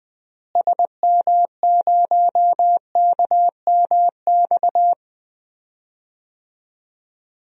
Morse code, heard as a keyed tone, about 20 wpm, 700 Hz. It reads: SM0KMX
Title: DXCC-100 Call Signs - 6 Characters - Repeated 2x - 20wpm